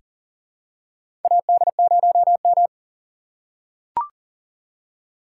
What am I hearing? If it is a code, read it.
Morse code, heard as a keyed tone, about 40 wpm, 700 Hz. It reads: AD0M
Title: USA Call Signs - 4 Characters - 4x Character Spacing - 40wpm